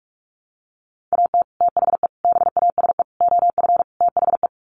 Morse code, heard as a keyed tone, about 44 wpm, 700 Hz. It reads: AT THE BASE OF THE
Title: Sets of 5 Words 45wpm